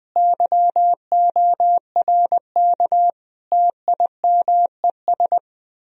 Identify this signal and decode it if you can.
Morse code, heard as a keyed tone, about 20 wpm, 700 Hz. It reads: YORK TIMES
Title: Speed-Racing - Sets of 5 Words 20wpm